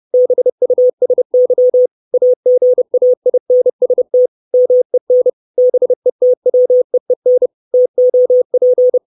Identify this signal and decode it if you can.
Morse code, heard as a keyed tone, about 30 wpm, 500 Hz. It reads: BUSY AGAINST MEN BETWEEN TOP
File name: 632 Words 30WPM